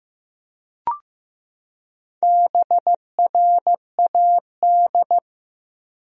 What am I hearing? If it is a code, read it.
Morse code, heard as a keyed tone, about 15 wpm, 700 Hz. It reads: BRAD
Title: USA Common Names - Max 4 Letters - 15wpm